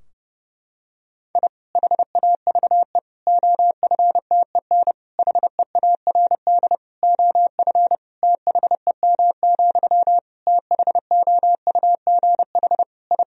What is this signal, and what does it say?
Morse code, heard as a keyed tone, about 32 wpm, 700 Hz. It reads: I HAVE OFTEN HEARD OF THEM, THOUGH I
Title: Sentences from Top 1000 Words - Review Part 2 - 30wpm